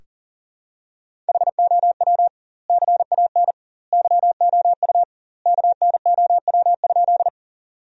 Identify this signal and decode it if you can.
Morse code, heard as a keyed tone, about 37 wpm, 700 Hz. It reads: HOW CAN YOU KNOW?
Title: 2x Word-Spacing - Sentences from Top 100 Words - 40wpm